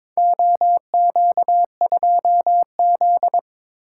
Morse code, heard as a keyed tone, about 22 wpm, 700 Hz. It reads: OQ2Z
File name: practice / 22wpm / Four Letter-Number Combinations - 22wpm